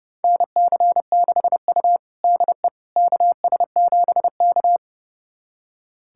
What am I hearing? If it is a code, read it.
Morse code, heard as a keyed tone, about 29 wpm, 700 Hz. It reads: NC6U DE KS7K
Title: USA Callsign DE Callsign - 4 Characters - 30wpm